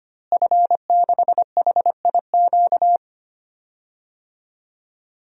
Morse code, heard as a keyed tone, about 25 wpm, 700 Hz. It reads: F6HIQ
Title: DXCC-100 Call Signs - 5 Characters - Repeated 2x - 25wpm